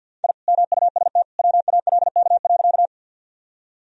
Morse code, heard as a keyed tone, about 46 wpm, 700 Hz. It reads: I MUST WALK.
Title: Sentences from Top 1000 Words - Review Part 1 - 50wpm